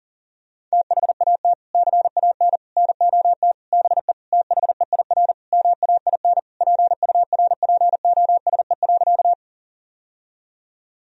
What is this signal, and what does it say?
Morse code, heard as a keyed tone, about 37 wpm, 700 Hz. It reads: THAT CAN NOT BE THEIR MAIN PURPOSE.
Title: Sentences from Top 1000 Words - Review Part 2 - 40wpm